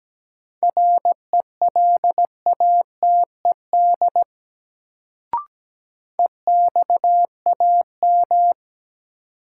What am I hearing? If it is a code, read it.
Morse code, heard as a keyed tone, about 17 wpm, 700 Hz. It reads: RELATED EXAM
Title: Sentences from Top 1000 Words - Review Part 5 - New Word Drill - 17wpm